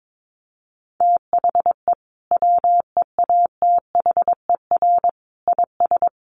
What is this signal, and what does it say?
Morse code, heard as a keyed tone, about 22 wpm, 700 Hz. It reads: THE WEATHER IS
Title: Sentences from Top 800 Words 22wpm